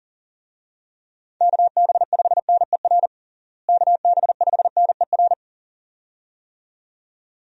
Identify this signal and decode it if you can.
Morse code, heard as a keyed tone, about 40 wpm, 700 Hz. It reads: KB5NER KB5NER
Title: USA Call Signs - 6 Characters - Repeated 2x - 40wpm